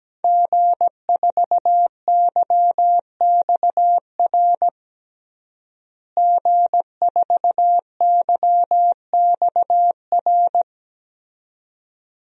Morse code, heard as a keyed tone, about 17 wpm, 700 Hz. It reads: G4YXR G4YXR
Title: DXCC-100 Call Signs - 5 Characters - Repeated 2x - 17wpm